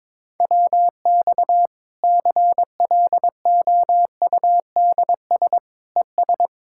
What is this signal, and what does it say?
Morse code, heard as a keyed tone, about 22 wpm, 700 Hz. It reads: WX CLOUDS ES
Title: K1USN SST QSO Practice 22wpm